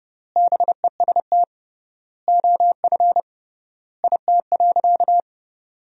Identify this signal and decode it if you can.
Morse code, heard as a keyed tone, about 31 wpm, 700 Hz. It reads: BEST OF IT.
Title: 3x Word-Spacing - Sentences from Top 400 Words - 30wpm